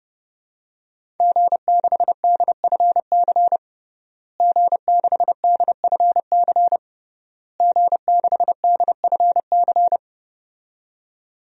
Morse code, heard as a keyed tone, about 30 wpm, 700 Hz. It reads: G6DFC G6DFC G6DFC
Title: DXCC-100 Call Signs - 5 Characters - Repeated 3x - 30wpm